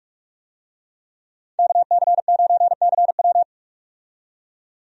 Morse code, heard as a keyed tone, about 42 wpm, 700 Hz. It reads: KC9CW
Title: USA Call Signs - 5 Characters - 45wpm